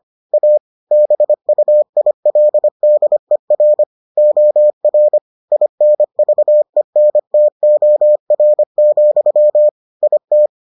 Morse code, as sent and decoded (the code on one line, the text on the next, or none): .- / -... ..- .. .-.. -.. .-. / --- .-. / .. -. ...- . -. - --- .-. --..-- / .. -
A BUILDER OR INVENTOR, IT